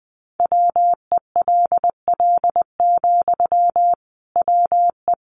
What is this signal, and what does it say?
Morse code, heard as a keyed tone, about 20 wpm, 700 Hz. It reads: WELL, WE
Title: Sentences from Top 1100 Words 20wpm